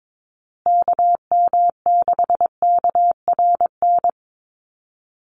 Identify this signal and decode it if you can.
Morse code, heard as a keyed tone, about 22 wpm, 700 Hz. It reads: KM6KRN
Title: USA Call Signs Repeated 2x 22wpm